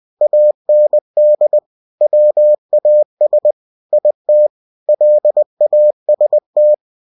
Morse code, heard as a keyed tone, about 20 wpm, 600 Hz. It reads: AND WAS IT LAST
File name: cw-books / winnie-the-pooh / winnie-20wpm-page-0108